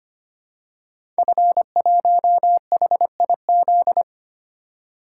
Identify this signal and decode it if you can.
Morse code, heard as a keyed tone, about 25 wpm, 700 Hz. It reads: F1HIZ
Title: Code Groups - 5 Letter:Numbers - 25wpm